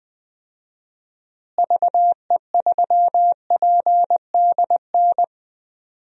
Morse code, heard as a keyed tone, about 20 wpm, 700 Hz. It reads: VE3PDN
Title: DXCC-100 Call Signs - 6 Characters - Repeated 3x - 20wpm